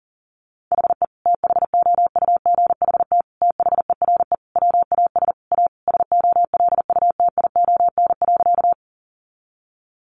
Morse code, heard as a keyed tone, about 40 wpm, 700 Hz. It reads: HE THOUGHT THERE WAS A SOLUTION.
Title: Sentences from Top 1300 Words 40wpm